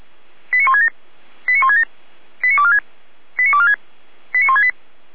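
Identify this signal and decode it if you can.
Five-tone selective call (ZVEI), heard as quick runs of five tones, tones of about 70 ms. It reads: 89167 89167 89246 89246 89168
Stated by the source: ZVEI five-tone call received by SDR, from the feuerwehr-orlamuende.org fire brigade audio archive.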